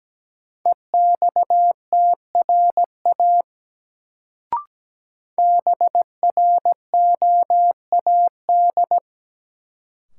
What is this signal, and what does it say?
Morse code, heard as a keyed tone, about 17 wpm, 700 Hz. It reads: EXTRA BROAD
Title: Sentences from Top 1000 Words - New Word Drill - 17wpm